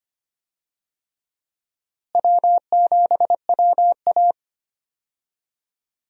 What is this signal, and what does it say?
Morse code, heard as a keyed tone, about 25 wpm, 700 Hz. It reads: W7WA
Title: USA Call Signs - 4 Characters - Repeated 3x - 25wpm